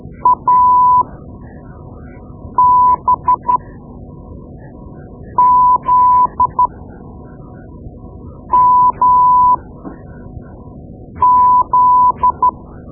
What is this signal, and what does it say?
Morse code, heard as a keyed tone, about 13 wpm, spaced as for 4 wpm, 1000 Hz. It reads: ABZMZ
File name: Morse Lessons / Morse Lessons / Tape-2B